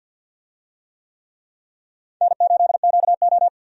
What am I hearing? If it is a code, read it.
Morse code, heard as a keyed tone, about 49 wpm, 700 Hz. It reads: N8QO
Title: USA Call Signs - 4 Characters - 50wpm